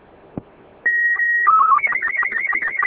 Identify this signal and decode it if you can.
SSTV (slow-scan television), Robot 36.